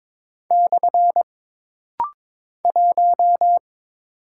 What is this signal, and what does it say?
Morse code, heard as a keyed tone, about 22 wpm, 700 Hz. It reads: / 1
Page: This rapid-fire timing is geared to push your ICR (Instant Character Recognition) skills.